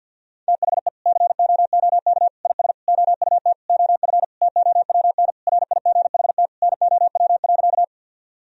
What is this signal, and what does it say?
Morse code, heard as a keyed tone, about 52 wpm, 700 Hz. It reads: THE COOK IS OUT OF TOWN RIGHT NOW.
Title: Sentences from Top 1000 Words - Review Part 4 - 50wpm